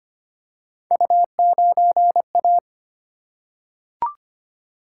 Morse code, heard as a keyed tone, about 25 wpm, 700 Hz. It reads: U9A
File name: Three Letter-Number Combinations - 25wpm